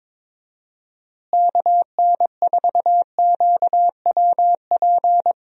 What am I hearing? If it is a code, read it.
Morse code, heard as a keyed tone, about 21 wpm, 700 Hz. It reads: KN4QWP
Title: USA Call Signs - 6 Characters - Repeated 3x - 22wpm